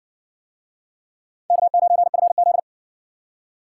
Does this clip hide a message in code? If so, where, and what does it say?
Yes: Morse code from 1.5 to 2.6 s, heard as a keyed tone, about 60 wpm, 700 Hz. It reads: K9FZ